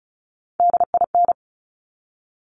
Morse code, heard as a keyed tone, about 36 wpm, 690 Hz. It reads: DIN
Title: Top Three Letter Combinations in Words 35wpm